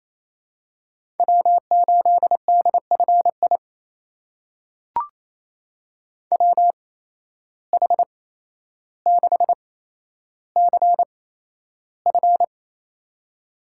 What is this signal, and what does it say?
Morse code, heard as a keyed tone, about 28 wpm, spaced as for 12 wpm, 700 Hz. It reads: W8DFI WH6CF
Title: USA Call Signs - 8x Character Spacing - 28wpm